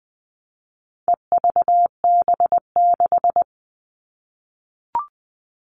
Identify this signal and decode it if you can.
Morse code, heard as a keyed tone, about 20 wpm, 700 Hz. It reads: EVB6
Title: All Cut Numbers - 4-Digit 20wpm